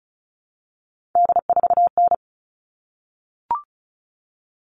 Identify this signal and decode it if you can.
Morse code, heard as a keyed tone, about 34 wpm, 700 Hz. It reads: D4N
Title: All Cut Numbers - 3-Digit 35wpm